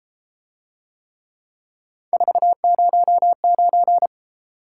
Morse code, heard as a keyed tone, about 31 wpm, 700 Hz. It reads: 409